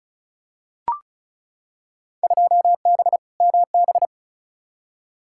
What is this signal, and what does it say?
Morse code, heard as a keyed tone, about 36 wpm, 700 Hz. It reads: 2B MB
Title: Field Day Exchanges Encore - 35wpm